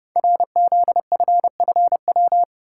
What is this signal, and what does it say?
Morse code, heard as a keyed tone, about 30 wpm, 700 Hz. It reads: RZFFW
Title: Code Groups - 5 Letters - 30wpm